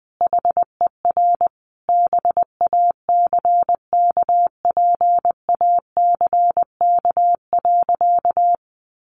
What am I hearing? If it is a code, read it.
Morse code, heard as a keyed tone, about 20 wpm, 700 Hz. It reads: HER BACKPACK.
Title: Additional Easy Phrases 20wpm